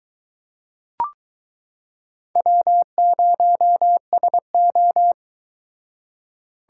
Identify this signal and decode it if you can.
Morse code, heard as a keyed tone, about 22 wpm, 700 Hz. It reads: W0SO